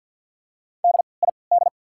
Morse code, heard as a keyed tone, about 53 wpm, spaced as for 29 wpm, 690 Hz. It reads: DID
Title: 3x Character Spacing - Top 100 Words - 50wpm